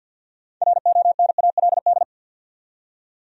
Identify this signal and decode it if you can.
Morse code, heard as a keyed tone, about 52 wpm, 700 Hz. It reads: RONALD
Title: USA Common Names - 50wpm